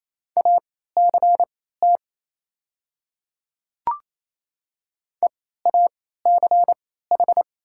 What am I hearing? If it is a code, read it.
Morse code, heard as a keyed tone, about 27 wpm, spaced as for 16 wpm, 700 Hz. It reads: ACT EACH
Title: 3x Character Spacing - Top 200 Words - 28wpm